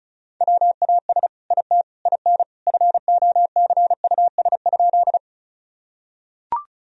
Morse code, heard as a keyed tone, about 34 wpm, 700 Hz. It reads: WAS IT IN FOCUS?
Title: Sentences from Top 1000 Words - Review Part 4 - 35wpm